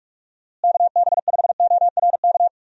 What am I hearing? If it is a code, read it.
Morse code, heard as a keyed tone, about 46 wpm, 700 Hz. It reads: KB5ORK
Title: USA Call Signs - 6 Characters - Repeated 2x - 45wpm